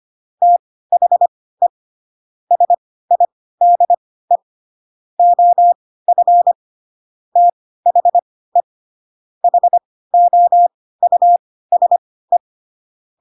Morse code, heard as a keyed tone, about 25 wpm, spaced as for 16 wpm, 700 Hz. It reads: THE SIDE OF THE HOUSE
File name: Pinocchio / Pinocchio ch7 10 25 ik1hge